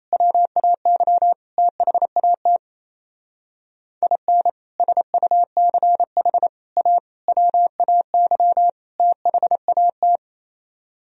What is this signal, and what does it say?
Morse code, heard as a keyed tone, about 36 wpm, 700 Hz. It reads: WAY THAT IN SUCH A WAY THAT